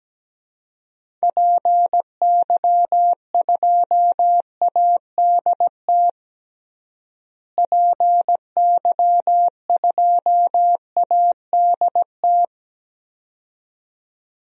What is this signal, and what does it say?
Morse code, heard as a keyed tone, about 17 wpm, 700 Hz. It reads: PY2ADT PY2ADT
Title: DXCC-100 Call Signs - 6 Characters - Repeated 2x - 17wpm